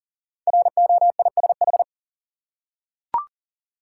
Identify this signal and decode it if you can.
Morse code, heard as a keyed tone, about 42 wpm, 690 Hz. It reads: ROISH
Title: Code Groups - 5 Letters - 40wpm